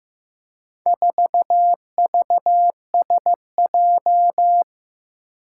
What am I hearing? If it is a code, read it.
Morse code, heard as a keyed tone, about 15 wpm, 700 Hz. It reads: 4VSJ
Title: Four Challenging Letter-Number Combinations 15wpm